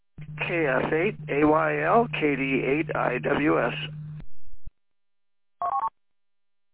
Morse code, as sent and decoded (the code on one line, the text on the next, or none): none